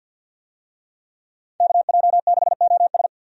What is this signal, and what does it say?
Morse code, heard as a keyed tone, about 50 wpm, 700 Hz. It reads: KJ6OS
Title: USA Call Signs - 5 Characters - 50wpm